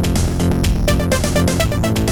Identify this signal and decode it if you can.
There is no code.